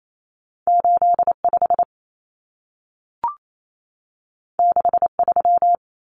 Morse code, heard as a keyed tone, about 28 wpm, 700 Hz. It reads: 85 63
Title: Numbers - 2-Digit 28wpm